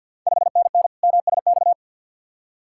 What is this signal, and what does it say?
Morse code, heard as a keyed tone, about 50 wpm, 700 Hz. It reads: 5NN MSX
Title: Ontario QSO Party Exchange 50wpm